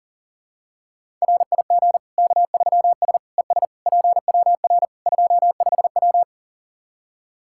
Morse code, heard as a keyed tone, about 40 wpm, 700 Hz. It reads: RIG K3S ES PWR 25W
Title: Practice QSO - Exchange 2 - Info - 40wpm